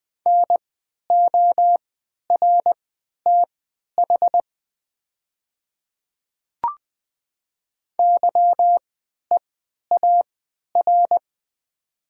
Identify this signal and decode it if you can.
Morse code, heard as a keyed tone, about 20 wpm, spaced as for 11 wpm, 700 Hz. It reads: NORTH YEAR
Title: 3x Character Spacing - Top 300 Words - 20wpm